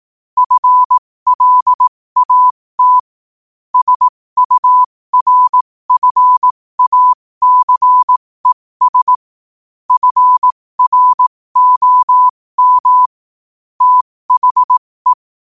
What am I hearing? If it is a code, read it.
Morse code, heard as a keyed tone, about 18 wpm, spaced as for 15 wpm, 1000 Hz. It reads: FLAT SURFACES FROM THE
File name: qotdfiles / 15wpm